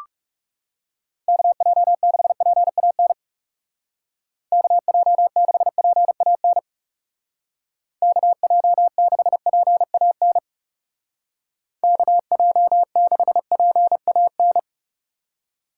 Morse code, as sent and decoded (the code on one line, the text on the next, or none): -.- .--- -.... .--. .- -. / -.- .--- -.... .--. .- -. / -.- .--- -.... .--. .- -. / -.- .--- -.... .--. .- -.
KJ6PAN KJ6PAN KJ6PAN KJ6PAN